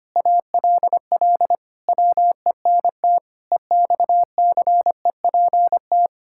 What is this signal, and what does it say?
Morse code, heard as a keyed tone, about 25 wpm, 700 Hz. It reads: ALL WENT EXCEPT
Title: Sentences from Top 1600 Words 25wpm